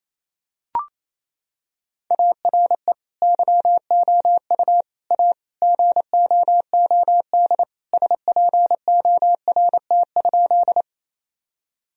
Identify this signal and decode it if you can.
Morse code, heard as a keyed tone, about 28 wpm, 700 Hz. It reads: ARE YOU A GOOD SPORT?
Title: Sentences from Top 1000 Words - Review Part 4 - 28wpm